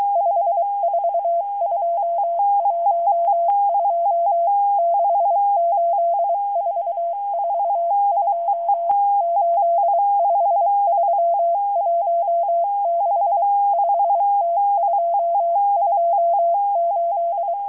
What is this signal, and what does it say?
Morse code, heard as a keyed tone, about 23 wpm, 680 Hz. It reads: 54212 68442 85316 5T228